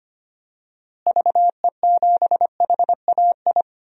Morse code, heard as a keyed tone, about 25 wpm, 700 Hz. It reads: VE7HAI
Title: DXCC-100 Call Signs - 6 Characters - Repeated 2x - 25wpm